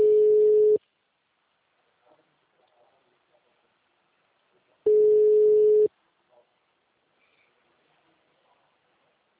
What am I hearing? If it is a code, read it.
Ringback tone (Europe, 425 Hz).
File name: out- -2035- - - .59